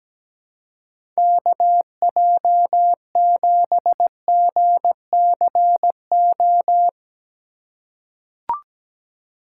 Morse code, heard as a keyed tone, about 17 wpm, 700 Hz. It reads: KJ7GCO
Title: USA Call Signs - 6x Character Spacing - 17wpm